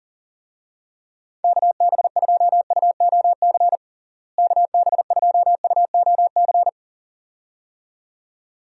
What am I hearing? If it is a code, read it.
Morse code, heard as a keyed tone, about 40 wpm, 700 Hz. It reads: KB2UOC KB2UOC
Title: USA Call Signs - 6 Characters - Repeated 2x - 40wpm